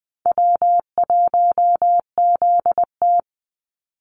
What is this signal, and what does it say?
Morse code, heard as a keyed tone, about 20 wpm, 700 Hz. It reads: W1ZT
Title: Call Signs Repeated 2x - 20wpm